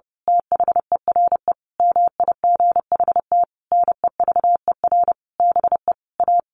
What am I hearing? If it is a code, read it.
Morse code, heard as a keyed tone, about 30 wpm, 700 Hz. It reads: THERE MIGHT NEVER BE A
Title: A Sound Of Thunder 30wpm